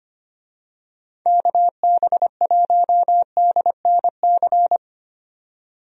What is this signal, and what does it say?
Morse code, heard as a keyed tone, about 25 wpm, 700 Hz. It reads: KB1DNC
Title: USA Call Signs - 6 Characters - 25wpm